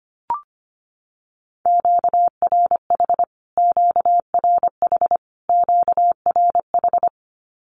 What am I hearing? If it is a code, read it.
Morse code, heard as a keyed tone, about 25 wpm, 700 Hz. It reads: QRH QRH QRH
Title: Q-Codes Repeated 3x 25wpm